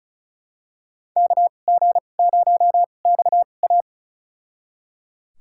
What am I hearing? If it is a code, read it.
Morse code, heard as a keyed tone, about 36 wpm, spaced as for 25 wpm, 700 Hz. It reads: KG0XA
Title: USA Call Signs - 5 Characters - 2x Character Spacing - 35wpm